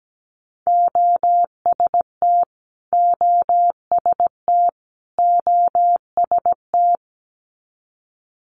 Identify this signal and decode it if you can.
Morse code, heard as a keyed tone, about 17 wpm, 700 Hz. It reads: OST OST OST
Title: Multi-letter Phonemes Repeated 3x 17wpm